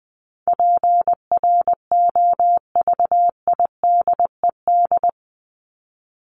Morse code, heard as a keyed tone, about 20 wpm, 700 Hz. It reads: PROVIDED
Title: Top 1751-2000 Words 20wpm